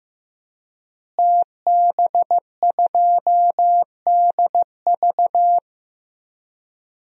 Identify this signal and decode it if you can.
Morse code, heard as a keyed tone, about 15 wpm, 700 Hz. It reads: TB2DV
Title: DXCC-100 Call Signs - 5 Characters - Repeated 2x - 15wpm